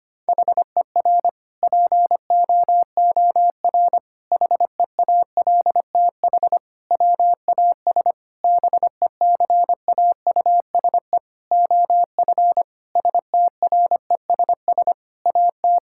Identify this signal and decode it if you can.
Morse code, heard as a keyed tone, about 25 wpm, 700 Hz. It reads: HER POOR HEALTH WAS BECAUSE OF STRESS AT
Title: Sentences from Top 1000 Words - Review Part 6 - 25wpm